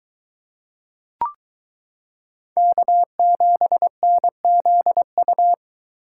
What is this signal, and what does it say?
Morse code, heard as a keyed tone, about 22 wpm, 700 Hz. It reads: K7NZU